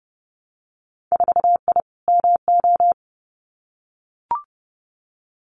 Morse code, heard as a keyed tone, about 30 wpm, 700 Hz. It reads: 4I MO
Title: Winter Field Day Exchanges 30wpm